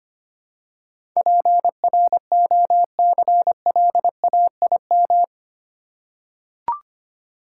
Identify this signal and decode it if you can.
Morse code, heard as a keyed tone, about 25 wpm, 700 Hz. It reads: PROCLAIM
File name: Eight Letter Words - 25wpm